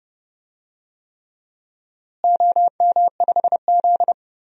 Morse code, heard as a keyed tone, about 30 wpm, 700 Hz. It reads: OM5Z
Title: DXCC-100 Call Signs - 4 Characters - Repeated 3x - 30wpm